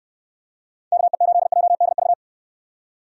Morse code, heard as a keyed tone, about 67 wpm, 700 Hz. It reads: KE8JDV